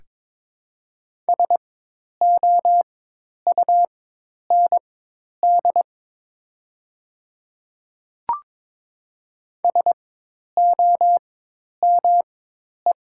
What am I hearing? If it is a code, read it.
Morse code, heard as a keyed tone, about 22 wpm, spaced as for 10 wpm, 700 Hz. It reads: SOUND SOME
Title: 4x Character Spacing - Top 100 Words - 22wpm